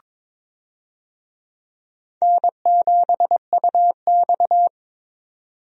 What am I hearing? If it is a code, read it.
Morse code, heard as a keyed tone, about 22 wpm, 700 Hz. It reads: N7UX
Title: USA Call Signs - 4 Characters - 22wpm